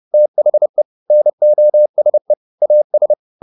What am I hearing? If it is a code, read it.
Morse code, heard as a keyed tone, about 30 wpm, 600 Hz. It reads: THE NOSE AS